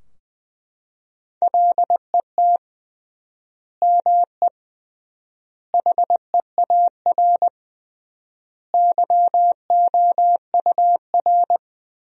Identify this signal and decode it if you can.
Morse code, heard as a keyed tone, about 20 wpm, 700 Hz. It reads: LET ME HEAR YOUR
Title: 3x Word-Spacing - Sentences from Top 300 Words - 20wpm